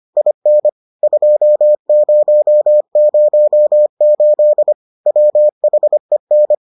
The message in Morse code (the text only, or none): IN 2008 WHEN